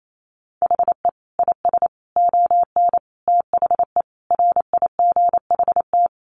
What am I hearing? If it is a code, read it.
Morse code, heard as a keyed tone, about 28 wpm, 700 Hz. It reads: HE IS ON THE RIGHT
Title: Sentences from Top 1100 Words 28wpm